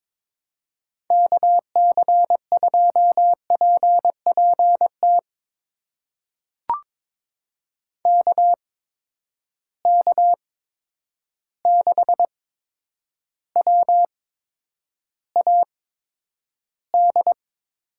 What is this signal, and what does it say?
Morse code, heard as a keyed tone, about 22 wpm, spaced as for 9 wpm, 700 Hz. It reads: KC2PPT KK6WAD
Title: USA Call Signs - 6 Characters - 8x Character Spacing - 22wpm